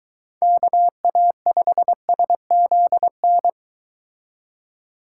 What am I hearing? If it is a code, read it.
Morse code, heard as a keyed tone, about 22 wpm, 700 Hz. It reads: KA5SZN